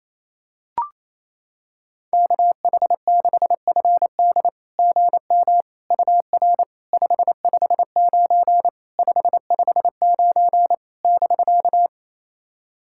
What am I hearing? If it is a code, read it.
Morse code, heard as a keyed tone, about 28 wpm, 700 Hz. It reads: KH6FD GM UR 559 559 <BK>
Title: SOTA QSO Practice 28wpm